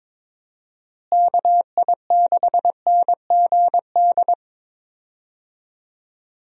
Morse code, heard as a keyed tone, about 22 wpm, 700 Hz. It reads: KI6NGD
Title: USA Call Signs - 6 Characters - Repeated 3x - 22wpm